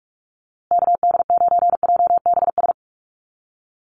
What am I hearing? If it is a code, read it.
Morse code, heard as a keyed tone, about 44 wpm, 700 Hz. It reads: KD9JBS
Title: USA Call Signs Repeated 3x 45wpm